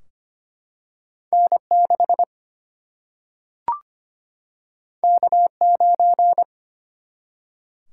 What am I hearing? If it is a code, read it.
Morse code, heard as a keyed tone, about 25 wpm, 700 Hz. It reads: N6 K9
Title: Single-Letter Single-Number 25wpm